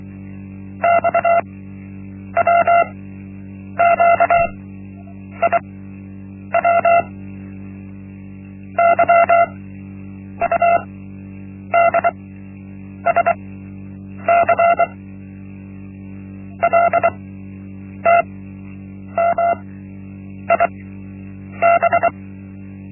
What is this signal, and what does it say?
Morse code, heard as a keyed tone, about 24 wpm, spaced as for 8 wpm, 680 Hz. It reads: XWQIW YUDSC LTMIB
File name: Linksmakalnis-1